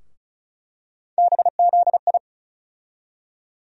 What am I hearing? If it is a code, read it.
Morse code, heard as a keyed tone, about 34 wpm, 700 Hz. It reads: BZI
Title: USA Call Sign Postfixes - 3 Letters 35wpm